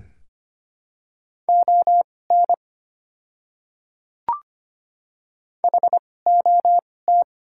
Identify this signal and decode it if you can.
Morse code, heard as a keyed tone, about 25 wpm, spaced as for 18 wpm, 700 Hz. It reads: ON HOT
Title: 2x Character Spacing - Top 500 Words - 25wpm